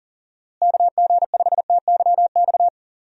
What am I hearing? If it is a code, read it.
Morse code, heard as a keyed tone, about 38 wpm, 700 Hz. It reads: KG5TYX
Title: USA Call Signs - 6 Characters - Repeated 3x - 40wpm